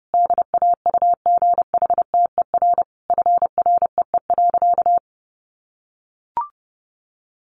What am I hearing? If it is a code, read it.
Morse code, heard as a keyed tone, about 30 wpm, 700 Hz. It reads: DAUGHTER FREE.
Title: Sentences from Top 1500 Words 30wpm